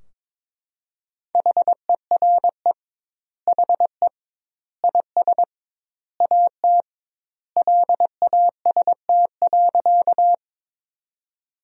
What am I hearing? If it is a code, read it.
Morse code, heard as a keyed tone, about 22 wpm, 700 Hz. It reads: HERE HE IS AT LAST.
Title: 2x Word-Spacing - Sentences from Top 300 Words - 22wpm